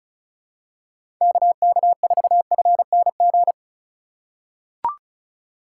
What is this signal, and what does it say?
Morse code, heard as a keyed tone, about 34 wpm, 700 Hz. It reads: KK4FNG
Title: USA Call Signs - 6 Characters - 3x Character Spacing - 35wpm